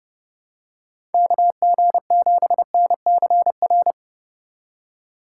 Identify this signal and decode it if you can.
Morse code, heard as a keyed tone, about 30 wpm, 700 Hz. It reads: KG7NCR